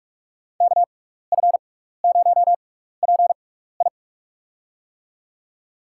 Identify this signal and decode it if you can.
Morse code, heard as a keyed tone, about 44 wpm, spaced as for 15 wpm, 700 Hz. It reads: KF0PI
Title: USA Call Signs - 5 Characters - 6x Character Spacing - 45wpm